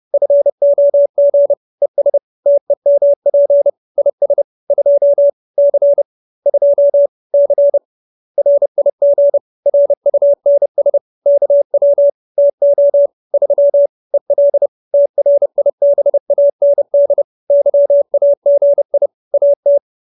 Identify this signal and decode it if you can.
Morse code, heard as a keyed tone, about 30 wpm, 570 Hz. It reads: FOG ES TEMP IS 2 C 2 C RIG RUNS KW TO 3 EL TRIBAND YAGI AT
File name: QSO319 30